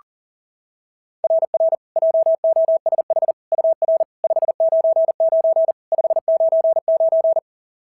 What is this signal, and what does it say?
Morse code, heard as a keyed tone, about 40 wpm, 690 Hz. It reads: RR JOSH UR 599 599
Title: SOTA QSO Practice 40wpm